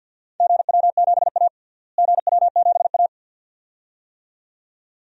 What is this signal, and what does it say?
Morse code, heard as a keyed tone, about 52 wpm, 700 Hz. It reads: GW7A GW7A
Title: DXCC-100 Call Signs - 4 Characters - Repeated 2x - 50wpm